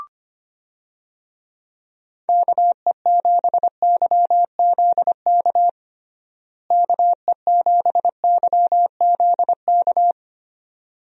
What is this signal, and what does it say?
Morse code, heard as a keyed tone, about 25 wpm, 700 Hz. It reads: KE7YZK KE7YZK
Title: USA Call Signs - 6 Characters - Repeated 2x - 25wpm